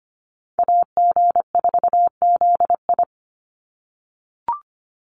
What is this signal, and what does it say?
Morse code, heard as a keyed tone, about 25 wpm, 700 Hz. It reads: AG4ZI
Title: USA Call Signs 25wpm